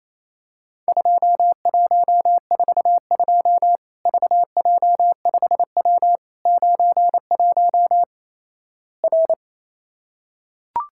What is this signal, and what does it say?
Morse code, heard as a keyed tone, about 27 wpm, 700 Hz. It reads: 2142 VJ5W 91 R
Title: Worked All Europe DX Contest - QTC Exchange - 28wpm